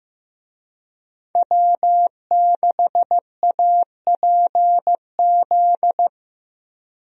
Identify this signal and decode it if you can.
Morse code, heard as a keyed tone, about 15 wpm, 700 Hz. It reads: W6APZ
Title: USA Call Signs - 5 Characters - Repeated 3x - 15wpm